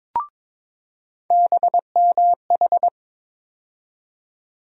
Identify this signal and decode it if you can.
Morse code, heard as a keyed tone, about 22 wpm, 700 Hz. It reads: BMH
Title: USA Call Sign Postfixes - 3 Letters 22wpm